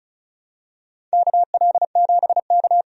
Morse code, heard as a keyed tone, about 34 wpm, 700 Hz. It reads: KL7K
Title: DXCC-100 Call Signs - 4 Characters - 35wpm